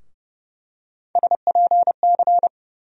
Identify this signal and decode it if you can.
Morse code, heard as a keyed tone, about 30 wpm, 700 Hz. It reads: SPC
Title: USA Call Sign Postfixes - 3 Letters 30wpm